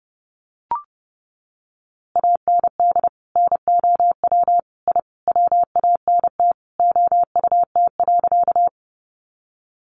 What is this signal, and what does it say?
Morse code, heard as a keyed tone, about 30 wpm, 700 Hz. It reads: AND NOW I WANT OUT.
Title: A Sound Of Thunder 30wpm